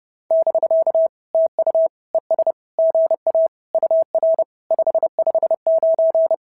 Morse code, heard as a keyed tone, about 28 wpm, 660 Hz. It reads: <BK> TU ES GA UR 559
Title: SOTA QSO Practice 30wpm